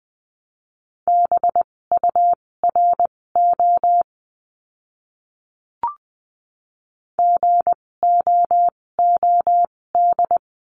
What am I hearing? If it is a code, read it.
Morse code, heard as a keyed tone, about 20 wpm, spaced as for 16 wpm, 700 Hz. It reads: BURO GOOD
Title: 256 Lesson 42a - Words 12-20wpm